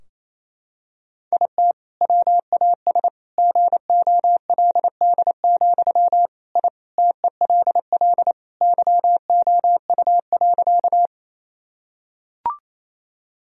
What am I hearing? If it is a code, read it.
Morse code, heard as a keyed tone, about 28 wpm, 700 Hz. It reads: IT WAS GOLD, I TELL YOU.
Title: Sentences from Top 1000 Words - Review Part 3 - 28wpm